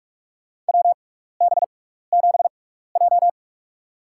Morse code, heard as a keyed tone, about 44 wpm, spaced as for 15 wpm, 700 Hz. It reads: WB7J
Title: USA Call Signs - 4 Characters - 6x Character Spacing - 45wpm